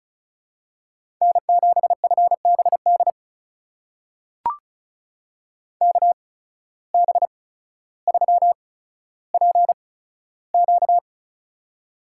Morse code, heard as a keyed tone, about 36 wpm, spaced as for 15 wpm, 700 Hz. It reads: N7FBD KB3PQ
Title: USA Call Signs - 5 Characters - 8x Character Spacing - 35wpm